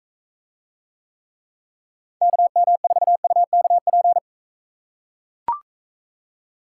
Morse code, heard as a keyed tone, about 41 wpm, 700 Hz. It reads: KM4UKP